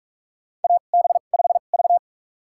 Morse code, heard as a keyed tone, about 46 wpm, spaced as for 33 wpm, 700 Hz. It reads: AB5V
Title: USA Call Signs - 4 Characters - 2x Character Spacing - 45wpm